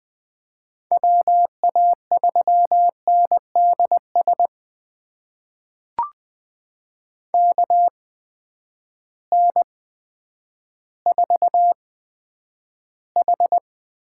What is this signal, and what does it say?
Morse code, heard as a keyed tone, about 20 wpm, spaced as for 10 wpm, 700 Hz. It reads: WA3NDS KN4H
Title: USA Call Signs - 6 Characters - 8x Character Spacing - 20wpm